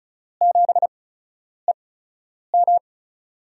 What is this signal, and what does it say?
Morse code, heard as a keyed tone, about 36 wpm, spaced as for 10 wpm, 700 Hz. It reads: 7EM
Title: USA Call Signs - 8x Character Spacing - 35wpm